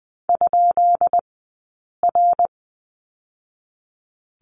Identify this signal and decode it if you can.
Morse code, heard as a keyed tone, about 20 wpm, 700 Hz. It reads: ? R
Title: 144 Lesson 24d - Letter y - Two Characters 20wpm